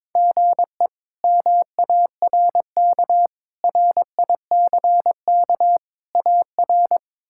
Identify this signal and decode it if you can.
Morse code, heard as a keyed tone, about 22 wpm, 700 Hz. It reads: GE MARK RICK AR
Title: K1USN SST QSO Practice - 22wpm